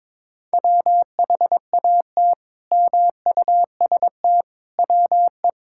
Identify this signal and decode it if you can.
Morse code, heard as a keyed tone, about 21 wpm, 700 Hz. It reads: WHAT MUST WE
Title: Sentences from Top 1000 Words - Review Part 6 - 22wpm